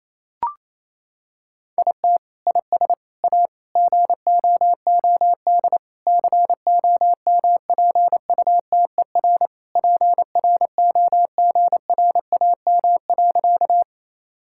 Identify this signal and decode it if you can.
Morse code, heard as a keyed tone, about 28 wpm, 700 Hz. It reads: IT IS A GOOD COMPUTER PROGRAM.
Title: Sentences from Top 1000 Words - Review Part 5 - 28wpm